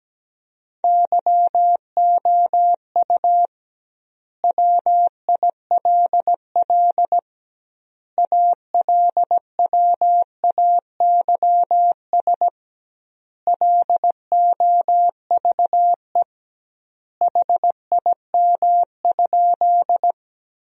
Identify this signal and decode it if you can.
Morse code, heard as a keyed tone, about 17 wpm, 700 Hz. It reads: YOU WILL ALWAYS LOVE HIM?
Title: 2x Word-Spacing - Sentences from Top 400 Words - 17wpm